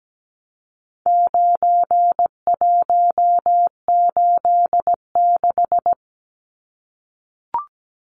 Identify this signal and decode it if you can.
Morse code, heard as a keyed tone, about 17 wpm, 700 Hz. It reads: 9186
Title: Numbers - 4-Digit 17wpm